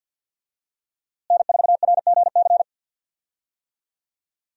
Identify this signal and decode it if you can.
Morse code, heard as a keyed tone, about 52 wpm, 700 Hz. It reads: N4RGC
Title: USA Call Signs - 5 Characters - Repeated 3x - 50wpm